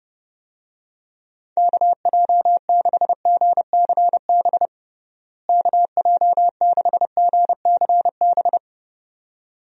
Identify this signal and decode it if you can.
Morse code, heard as a keyed tone, about 32 wpm, 700 Hz. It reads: KJ6GCB KJ6GCB
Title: USA Call Signs - 6 Characters - Repeated 2x - 30wpm